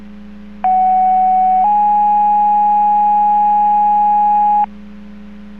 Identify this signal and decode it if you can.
Two-tone page, heard as a one-second tone, then a three-second tone, about 750 then 830 Hz. It is heard from 0.6 to 4.7 s.